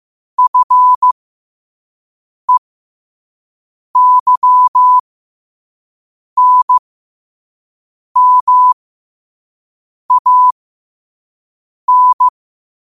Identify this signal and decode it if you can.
Morse code, heard as a keyed tone, about 15 wpm, spaced as for 5 wpm, 1000 Hz. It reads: FEYNMAN
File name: qotdfiles / 5wpm